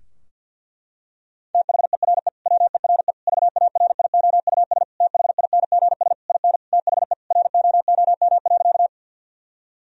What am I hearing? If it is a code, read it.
Morse code, heard as a keyed tone, about 50 wpm, 700 Hz. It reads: THERE WERE VARIOUS THINGS IN THE ROOM.
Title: Sentences from Top 1000 Words - Review Part 4 - 50wpm